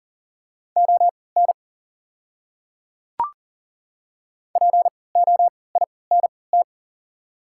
Morse code, heard as a keyed tone, about 38 wpm, spaced as for 22 wpm, 700 Hz. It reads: ON POINT
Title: 3x Character Spacing - Top 200 Words - 40wpm